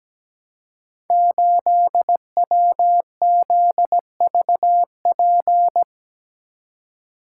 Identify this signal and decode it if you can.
Morse code, heard as a keyed tone, about 17 wpm, 700 Hz. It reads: 8WZVP
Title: Code Groups - 5 Letter:Numbers - 17wpm